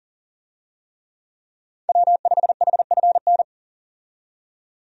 Morse code, heard as a keyed tone, about 40 wpm, 700 Hz. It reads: W5HFN
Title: USA Call Signs - 5 Characters - 40wpm